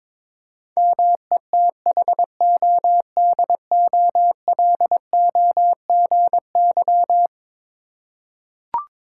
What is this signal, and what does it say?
Morse code, heard as a keyed tone, about 22 wpm, 700 Hz. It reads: METHODOLOGY
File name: practice / 22wpm / Eleven Letter Words - 22wpm